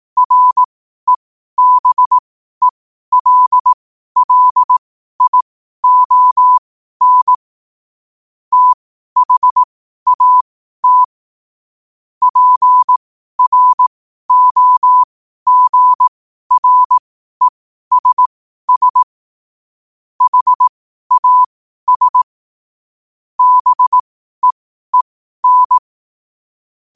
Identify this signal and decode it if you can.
Morse code, heard as a keyed tone, about 18 wpm, spaced as for 12 wpm, 1000 Hz. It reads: REBELLION THAT PROGRESS HAS BEEN
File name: qotdfiles / 12wpm